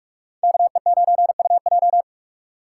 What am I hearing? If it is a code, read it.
Morse code, heard as a keyed tone, about 60 wpm, spaced as for 53 wpm, 700 Hz. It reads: KE9UJ
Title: Speed Racing - USA 2x2 Call Signs 45wpm